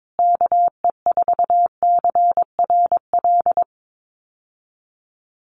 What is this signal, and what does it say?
Morse code, heard as a keyed tone, about 22 wpm, 700 Hz. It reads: KE4CRL
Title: USA Call Signs Repeated 3x 22wpm